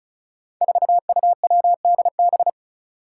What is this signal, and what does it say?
Morse code, heard as a keyed tone, about 36 wpm, 700 Hz. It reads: 4UWDB
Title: Five Challenging Letter-Number Combinations 35wpm